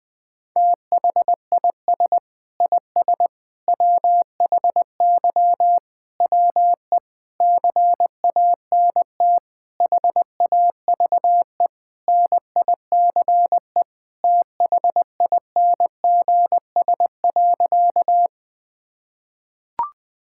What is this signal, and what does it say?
Morse code, heard as a keyed tone, about 21 wpm, 700 Hz. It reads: THIS IS WHY WE CANT HAVE NICE THINGS.
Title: Sentences with 8-Word Phrases - Spoken Beginning - 20wpm